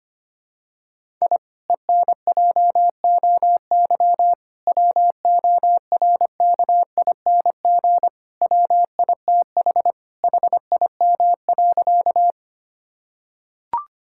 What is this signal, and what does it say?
Morse code, heard as a keyed tone, about 25 wpm, 700 Hz. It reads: I ENJOY WORKING WITH HIM.
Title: Sentences from Top 1000 Words - Review Part 5 - 25wpm